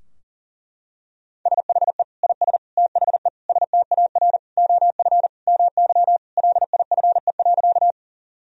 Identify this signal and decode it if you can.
Morse code, heard as a keyed tone, about 40 wpm, 700 Hz. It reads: SHE IS THE STAR OF MY LIFE.
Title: Sentences from Top 1000 Words - Review Part 2 - 40wpm